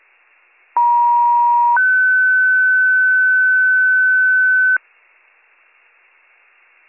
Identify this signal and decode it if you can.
Two-tone page, heard as a one-second tone, then a three-second tone, about 950 then 1500 Hz. It is heard from 0.8 to 4.8 s.